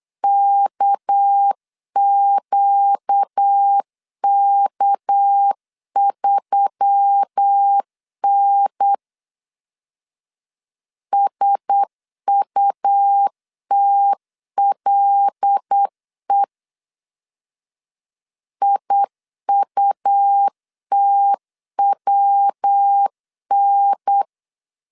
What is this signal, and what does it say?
Morse code, heard as a keyed tone, about 8 wpm, 800 Hz. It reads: KQK3N SUTLE IUTWN
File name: LEZIONI / 6 PKX0 / 6 PK 8wpm 3003